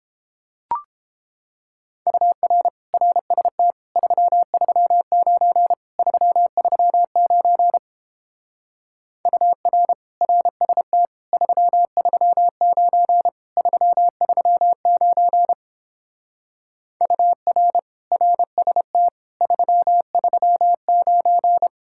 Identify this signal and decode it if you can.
Morse code, heard as a keyed tone, about 31 wpm, 700 Hz. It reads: UR RST 339 339 UR RST 339 339 UR RST 339